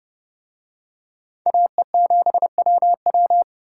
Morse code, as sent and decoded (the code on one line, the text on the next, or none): .- . --... .-- .--
AE7WW